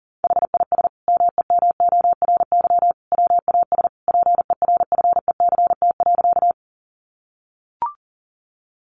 Morse code, heard as a keyed tone, about 40 wpm, 700 Hz. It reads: HIS MEMORY WAS PERFECT.
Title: Sentences from Top 1300 Words 40wpm